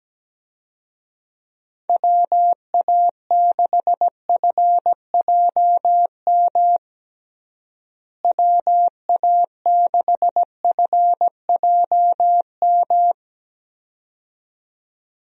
Morse code, heard as a keyed tone, about 17 wpm, 700 Hz. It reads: WA6FJM WA6FJM
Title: USA Call Signs - 6 Characters - Repeated 2x - 17wpm